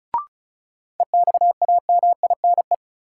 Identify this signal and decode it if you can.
Morse code, heard as a keyed tone, about 36 wpm, 700 Hz. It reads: EXAMINE
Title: Sentences from Top 1600 Words - New Word Drill Alt - 35wpm